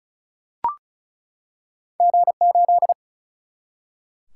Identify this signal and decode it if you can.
Morse code, heard as a keyed tone, about 36 wpm, 700 Hz. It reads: G 8
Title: Single-Letter Single-Number 35wpm